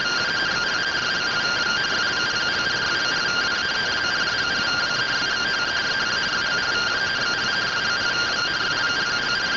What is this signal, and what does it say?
Two-tone teleprinter signal, about 170 Hz shift.
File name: findings / AXIAL fr45 3mm.gen-09 rtty rx